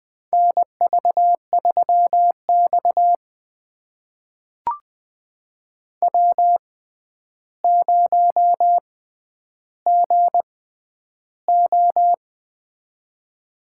Morse code, heard as a keyed tone, about 20 wpm, spaced as for 10 wpm, 700 Hz. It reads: NV3X W0GO
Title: USA Call Signs - 4 Characters - 6x Character Spacing - 20wpm